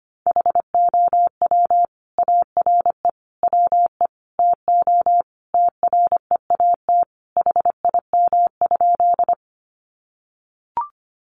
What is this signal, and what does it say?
Morse code, heard as a keyed tone, about 25 wpm, 700 Hz. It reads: HOW ARE WE TO TREAT HIM?
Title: Sentences from Top 1300 Words 25wpm